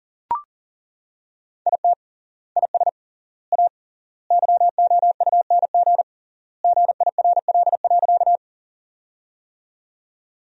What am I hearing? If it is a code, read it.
Morse code, heard as a keyed tone, about 40 wpm, 700 Hz. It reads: IT IS A YOUNG GIRL.
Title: 3x Word-Spacing - Sentences from Top 400 Words - 40wpm